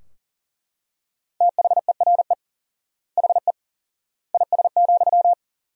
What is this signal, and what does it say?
Morse code, heard as a keyed tone, about 40 wpm, 700 Hz. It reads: THERE HE IS,
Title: 4x Word-Spacing - Sentences from Top 100 Words - 40wpm